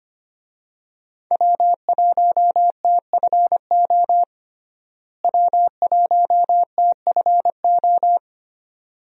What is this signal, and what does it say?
Morse code, heard as a keyed tone, about 25 wpm, 700 Hz. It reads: W1TFO W1TFO
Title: USA Call Signs - 5 Characters - Repeated 2x - 25wpm